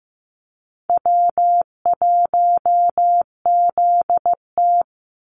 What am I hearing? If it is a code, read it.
Morse code, heard as a keyed tone, about 15 wpm, 700 Hz. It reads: W1ZT
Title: Call Signs Repeated 2x - 15wpm